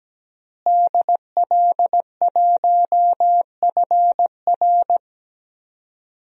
Morse code, heard as a keyed tone, about 17 wpm, 700 Hz. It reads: DL1FR
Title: DXCC-100 Call Signs - 5 Characters - Repeated 2x - 17wpm